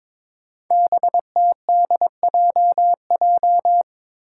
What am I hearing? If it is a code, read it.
Morse code, heard as a keyed tone, about 21 wpm, 700 Hz. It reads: BTDJJ
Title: Five Challenging Letter-Number Combinations 22wpm